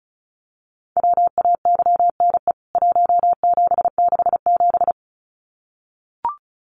Morse code, heard as a keyed tone, about 34 wpm, 700 Hz. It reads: WAYNE 1767
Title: CWT Exchange 35wpm